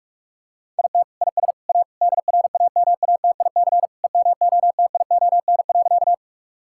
Morse code, heard as a keyed tone, about 46 wpm, 700 Hz. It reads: IT IS A DRAMATIC EMOTION.
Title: Sentences from Top 1600 Words 45wpm